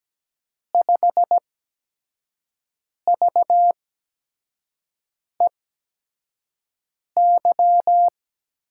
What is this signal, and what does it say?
Morse code, heard as a keyed tone, about 17 wpm, spaced as for 5 wpm, 700 Hz. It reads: 5VEY
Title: USA Call Signs - 8x Character Spacing - 17wpm